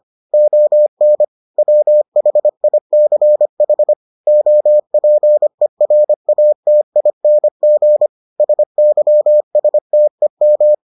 Morse code, heard as a keyed tone, about 25 wpm, 600 Hz. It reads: ON WHICH OPERATING SYSTEM